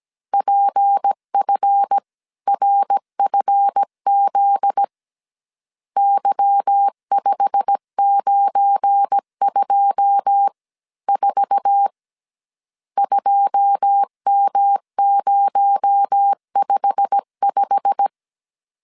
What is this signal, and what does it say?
Morse code, heard as a keyed tone, about 17 wpm, 800 Hz. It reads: PFRFZ Y5924 2M055
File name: CW / DET / 1 9 1500C S C 16wpm 2909